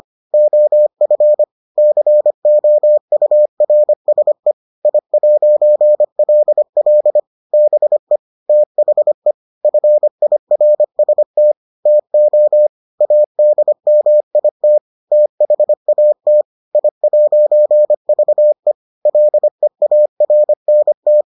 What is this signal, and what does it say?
Morse code, heard as a keyed tone, about 25 wpm, 600 Hz. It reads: OF COURSE I'LL BE THE FIRST TO ADMIT THAT I'VE LEARNT